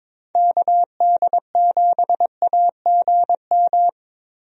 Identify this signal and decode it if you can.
Morse code, heard as a keyed tone, about 21 wpm, 700 Hz. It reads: KD7AGM
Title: USA Call Signs - 6 Characters - Repeated 2x - 22wpm